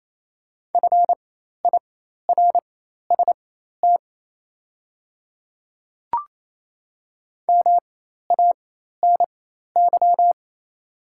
Morse code, heard as a keyed tone, about 28 wpm, spaced as for 13 wpm, 700 Hz. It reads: FIRST MANY
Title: 4x Character Spacing - Top 100 Words - 28wpm